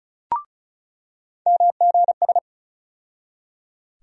Morse code, heard as a keyed tone, about 36 wpm, 700 Hz. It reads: MGS
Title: USA Call Sign Postfixes - 3 Letters 35wpm